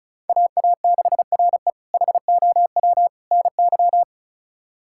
Morse code, heard as a keyed tone, about 36 wpm, 700 Hz. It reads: AA6RE HOW NY
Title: K1USN SST QSO Practice - 35wpm